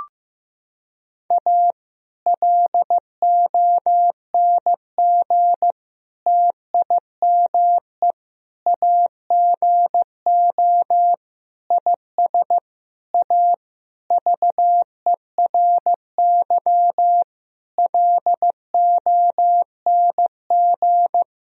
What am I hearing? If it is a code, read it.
Morse code, heard as a keyed tone, about 15 wpm, 700 Hz. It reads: A LONG TIME AGO IS A VERY LONG
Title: Sentences from Top 1000 Words - Review Part 3 - 15wpm